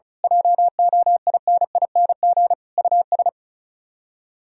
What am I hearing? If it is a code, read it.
Morse code, heard as a keyed tone, about 32 wpm, 700 Hz. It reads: JOINING US
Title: Speed-Racing - Sets of 5 Words 35wpm